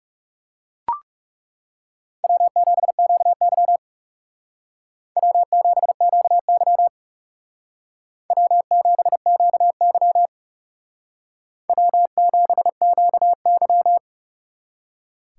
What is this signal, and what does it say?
Morse code, heard as a keyed tone, about 35 wpm, 700 Hz. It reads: W7QY W7QY W7QY W7QY
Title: Speed Racing - USA 1x2 Call Signs 30wpm